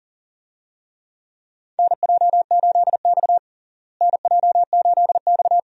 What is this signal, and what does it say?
Morse code, heard as a keyed tone, about 40 wpm, 700 Hz. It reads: NJ8X NJ8X
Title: USA Call Signs - 4 Characters - Repeated 2x - 40wpm